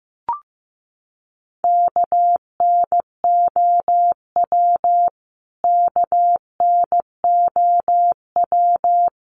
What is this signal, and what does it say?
Morse code, heard as a keyed tone, about 15 wpm, 700 Hz. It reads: KNOW KNOW
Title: Top 200 Words Repeated 15wpm